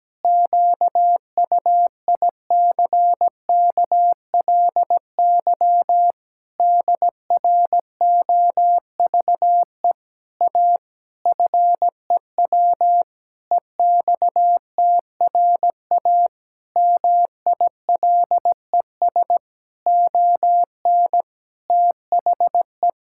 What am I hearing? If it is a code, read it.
Morse code, heard as a keyed tone, about 17 wpm, 700 Hz. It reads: QUICKLY DROVE A FEW EXTRA MILES ON THE
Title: Pangram Challenge v2 - 17wpm